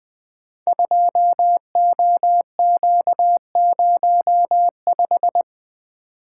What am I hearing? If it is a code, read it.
Morse code, heard as a keyed tone, about 20 wpm, 700 Hz. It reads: 2OQ05
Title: Code Groups - 5 Letter:Numbers - 20wpm